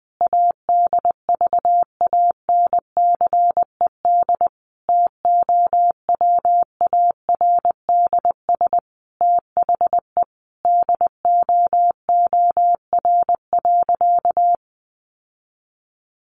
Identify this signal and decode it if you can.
Morse code, heard as a keyed tone, about 20 wpm, 700 Hz. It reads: ADVANCED TOWARDS THE DOOR.
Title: Sentences from Top 1200 Words 20wpm